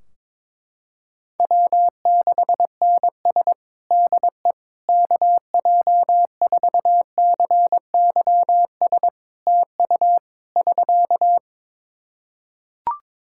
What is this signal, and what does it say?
Morse code, heard as a keyed tone, about 21 wpm, 700 Hz. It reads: W6NS DE KJ4CYS TU <SK>
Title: Practice QSO - Exchange 4 - Ending - 22wpm